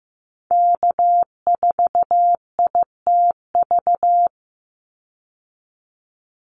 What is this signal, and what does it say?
Morse code, heard as a keyed tone, about 15 wpm, 700 Hz. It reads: K4ITV
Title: Call Signs Repeated 3x - 15wpm